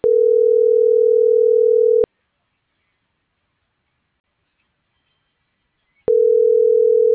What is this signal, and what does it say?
Ringback tone (North America), 440 + 480 Hz.